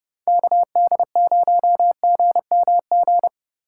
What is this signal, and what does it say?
Morse code, heard as a keyed tone, about 30 wpm, 700 Hz. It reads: KD0GMG
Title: USA Call Signs - 6 Characters - 2x Character Spacing - 30wpm